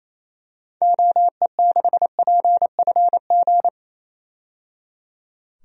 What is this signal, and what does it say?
Morse code, heard as a keyed tone, about 27 wpm, 700 Hz. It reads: OE6PFG
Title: DXCC-100 Call Signs - 6 Characters - 28wpm